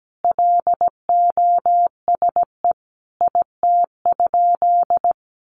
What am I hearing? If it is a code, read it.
Morse code, heard as a keyed tone, about 17 wpm, 700 Hz. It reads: LOSE IT?
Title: Sentences from Top 1100 Words 17wpm